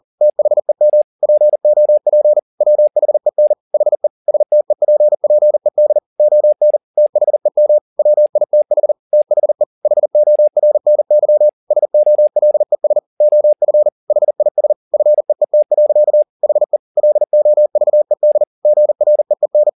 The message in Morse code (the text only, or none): THEM POP WHEN HE STEPPED ON THEM WITH THE HORNY SOLES OF HIS FEET. HE LOVED GREEN